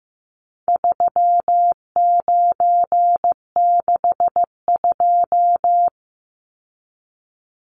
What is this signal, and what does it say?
Morse code, heard as a keyed tone, about 14 wpm, 700 Hz. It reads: 3962
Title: Numbers - 4-Digit 15wpm